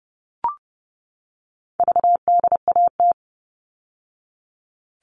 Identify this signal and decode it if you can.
Morse code, heard as a keyed tone, about 30 wpm, 700 Hz. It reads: VDAT
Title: All Cut Numbers - 4-Digit 30wpm